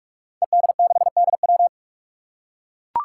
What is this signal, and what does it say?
Morse code, heard as a keyed tone, about 44 wpm, 700 Hz. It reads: ED6DW
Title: Five Challenging Letter-Number Combinations 45wpm